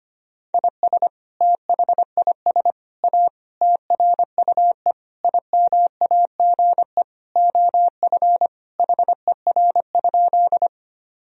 Morse code, heard as a keyed tone, about 25 wpm, 700 Hz. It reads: IS THIS A TRUE IMAGE OF HER?
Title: Sentences from Top 1000 Words - Review Part 4 - 25wpm